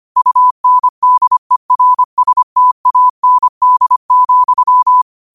Morse code, heard as a keyed tone, about 25 wpm, 1000 Hz. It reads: UNDERSTAND,
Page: New quotes every day in morse code at 25 Words per minute.